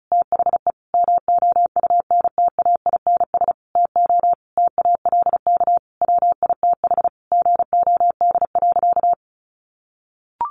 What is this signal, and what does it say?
Morse code, heard as a keyed tone, about 34 wpm, 700 Hz. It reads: THE MOUNTAINS TO TALK WITH GOD.
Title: A Sound Of Thunder 35wpm